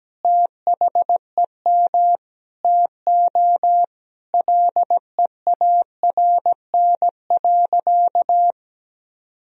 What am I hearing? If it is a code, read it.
Morse code, heard as a keyed tone, about 17 wpm, 700 Hz. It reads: THEM TO LEARN.
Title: Sentences from Top 1000 Words - Review Part 3 - 17wpm